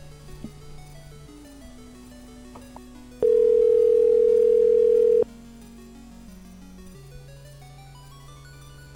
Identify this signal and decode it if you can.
Ringback tone (North America), 440 + 480 Hz.